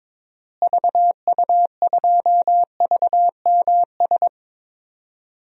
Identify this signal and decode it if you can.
Morse code, heard as a keyed tone, about 22 wpm, 700 Hz. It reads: VU2VMS
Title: DXCC-100 Call Signs - 6 Characters - Repeated 2x - 22wpm